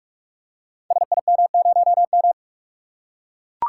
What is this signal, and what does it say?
Morse code, heard as a keyed tone, about 44 wpm, 700 Hz. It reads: SIM0M
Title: Code Groups - 5 Letter:Numbers - 45wpm